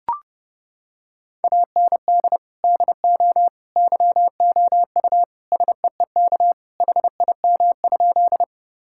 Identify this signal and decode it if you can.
Morse code, heard as a keyed tone, about 30 wpm, 700 Hz. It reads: AND DO YOU SEEK HIM?
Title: Sentences from Top 1600 Words 30wpm